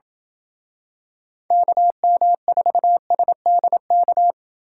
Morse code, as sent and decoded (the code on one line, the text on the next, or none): -.- -- ....- ... -.. -.-
KM4SDK